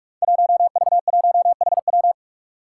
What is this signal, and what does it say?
Morse code, heard as a keyed tone, about 44 wpm, 700 Hz. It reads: 1V1HW
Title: Five Challenging Letter-Number Combinations 45wpm